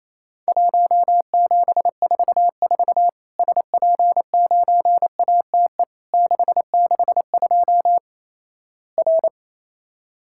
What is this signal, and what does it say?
Morse code, heard as a keyed tone, about 27 wpm, 700 Hz. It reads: 1744 SP9ATE 662 R
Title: Worked All Europe DX Contest - QTC Exchange - 28wpm